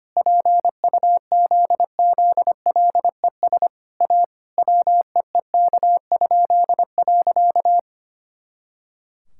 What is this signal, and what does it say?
Morse code, heard as a keyed tone, about 25 wpm, 700 Hz. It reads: PUZZLES A WEEK?.
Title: Pangram Challenge v2 - 25wpm